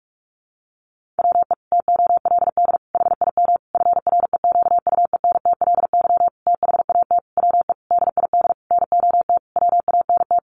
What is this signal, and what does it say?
Morse code, heard as a keyed tone, about 44 wpm, 700 Hz. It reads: WE TOLD HIM FREQUENTLY THAT WE DID NOT WANT
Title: Sentences from Top 1300 Words 45wpm